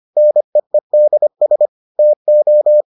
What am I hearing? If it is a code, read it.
Morse code, heard as a keyed tone, about 25 wpm, 600 Hz. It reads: NEEDS TO